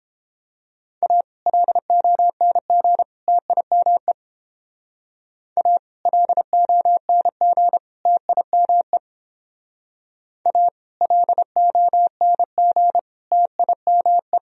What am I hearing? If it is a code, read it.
Morse code, heard as a keyed tone, about 30 wpm, 700 Hz. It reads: A LONG TIME A LONG TIME A LONG TIME